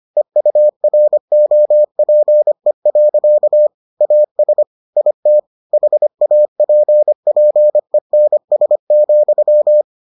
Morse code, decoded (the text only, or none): EUROPE. AS IT HAPPENS,